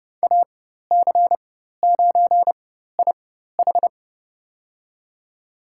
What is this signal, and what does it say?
Morse code, heard as a keyed tone, about 30 wpm, spaced as for 14 wpm, 700 Hz. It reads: AC9IH
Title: USA Call Signs - 4x Character Spacing - 30wpm